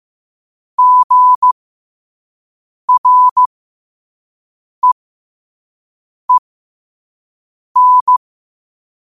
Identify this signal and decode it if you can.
Morse code, heard as a keyed tone, about 15 wpm, spaced as for 5 wpm, 1000 Hz. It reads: GREEN
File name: qotdfiles / 5wpm